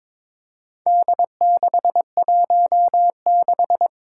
Morse code, heard as a keyed tone, about 23 wpm, 700 Hz. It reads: D616
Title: Four Challenging Letter-Number Combinations 22wpm